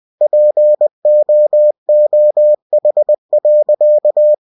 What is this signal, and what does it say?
Morse code, heard as a keyed tone, about 20 wpm, 600 Hz. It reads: POOH.
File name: cw-books / winnie-the-pooh / winnie-20wpm-page-0088